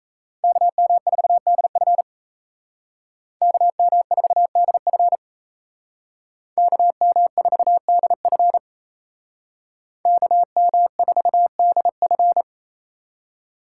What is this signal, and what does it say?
Morse code, heard as a keyed tone, about 34 wpm, 700 Hz. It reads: KM4DF KM4DF KM4DF KM4DF
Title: Speed Racing - USA 2x2 Call Signs 28wpm